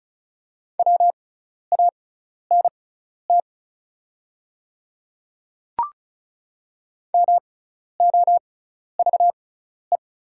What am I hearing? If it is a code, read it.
Morse code, heard as a keyed tone, about 36 wpm, spaced as for 12 wpm, 700 Hz. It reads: WANT MOVE
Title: 6x Character Spacing - Top 200 Words - 35wpm